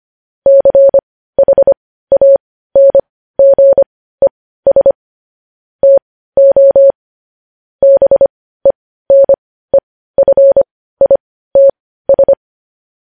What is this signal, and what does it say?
Morse code, heard as a keyed tone, about 25 wpm, spaced as for 15 wpm, 550 Hz. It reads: CHANGES TO BENEFITS
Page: news headlines in morse code at 15 wpm